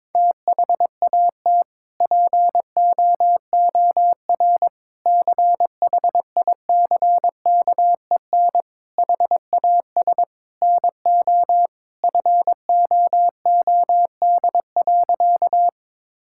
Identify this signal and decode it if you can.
Morse code, heard as a keyed tone, about 22 wpm, 700 Hz. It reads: THAT POOR CHICKEN HAS NO FOOD.
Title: Sentences from Top 1000 Words - Review Part 5 - 22wpm